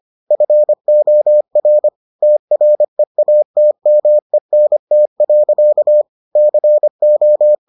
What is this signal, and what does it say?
Morse code, heard as a keyed tone, about 25 wpm, 600 Hz. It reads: FOR TREATMENT. CO